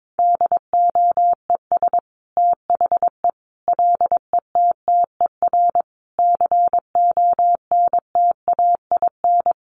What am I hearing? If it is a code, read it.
Morse code, heard as a keyed tone, about 22 wpm, 700 Hz. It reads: DOES THE LETTER CONTAIN
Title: Sentences from Top 1100 Words 22wpm